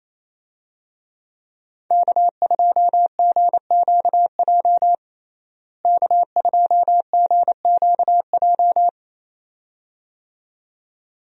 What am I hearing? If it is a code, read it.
Morse code, heard as a keyed tone, about 27 wpm, 700 Hz. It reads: K2GQJ K2GQJ
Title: USA Call Signs - 5 Characters - Repeated 2x - 28wpm